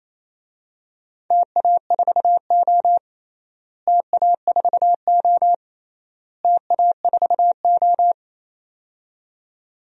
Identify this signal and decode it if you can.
Morse code, heard as a keyed tone, about 28 wpm, 700 Hz. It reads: TA4O TA4O TA4O
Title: DXCC-100 Call Signs - 4 Characters - Repeated 3x - 28wpm